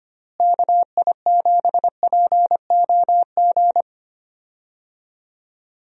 Morse code, heard as a keyed tone, about 25 wpm, 700 Hz. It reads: KI7POG
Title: USA Call Signs - 6 Characters - Repeated 3x - 25wpm